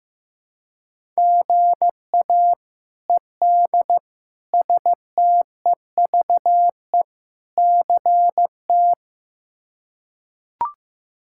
Morse code, heard as a keyed tone, about 15 wpm, 700 Hz. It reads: GA ED STEVE CT
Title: K1USN SST QSO Practice - 15wpm